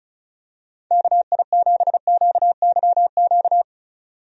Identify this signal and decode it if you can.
Morse code, heard as a keyed tone, about 36 wpm, 700 Hz. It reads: KI7QYQ
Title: USA Call Signs - 6 Characters - 35wpm